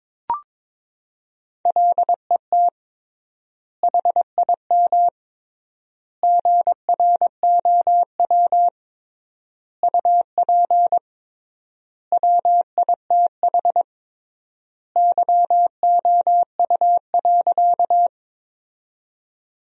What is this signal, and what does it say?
Morse code, heard as a keyed tone, about 22 wpm, 700 Hz. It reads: LET HIM GROW UP WITH YOU.
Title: 3x Word-Spacing - Sentences from Top 300 Words - 22wpm